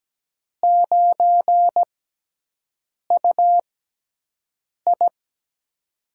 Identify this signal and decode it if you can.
Morse code, heard as a keyed tone, about 17 wpm, spaced as for 6 wpm, 700 Hz. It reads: 9UI
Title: USA Call Signs - 4 Characters - 6x Character Spacing - 17wpm